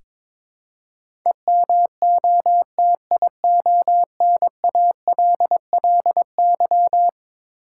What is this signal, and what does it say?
Morse code, heard as a keyed tone, about 22 wpm, 700 Hz. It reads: EMOTIONALLY
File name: practice / 22wpm / Eleven Letter Words - 22wpm